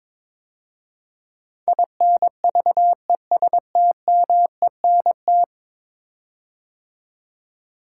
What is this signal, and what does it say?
Morse code, heard as a keyed tone, about 22 wpm, 700 Hz. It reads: INVESTMENT
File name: practice / 22wpm / 10 Letter Words - First 3 Letters Given - 22wpm